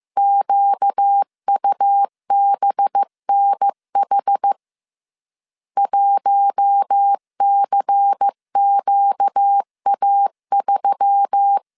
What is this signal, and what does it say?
Morse code, heard as a keyed tone, about 15 wpm, 800 Hz. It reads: QUBNH 1CQA3
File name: CW / DET / 1 6 1500C S A 14wpm 2921